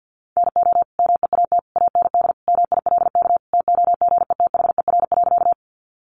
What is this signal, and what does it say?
Morse code, heard as a keyed tone, about 50 wpm, 700 Hz. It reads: NO MEAT AND MILK TOGETHER.
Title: Sentences from Top 1500 Words 50wpm